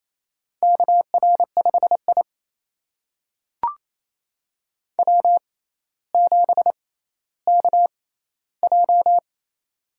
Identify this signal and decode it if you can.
Morse code, heard as a keyed tone, about 28 wpm, spaced as for 14 wpm, 700 Hz. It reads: KR5I W7KJ
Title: USA Call Signs - 4 Characters - 6x Character Spacing - 28wpm